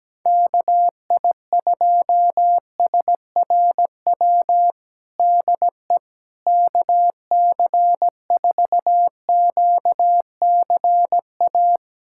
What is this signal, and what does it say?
Morse code, heard as a keyed tone, about 17 wpm, 700 Hz. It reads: KI2SRW DE KC4QCA
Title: USA Callsign DE Callsign 17wpm